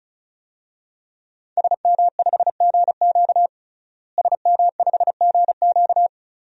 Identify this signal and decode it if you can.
Morse code, heard as a keyed tone, about 36 wpm, 700 Hz. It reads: SM5GQ SM5GQ
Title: DXCC-100 Call Signs - 5 Characters - Repeated 3x - 35wpm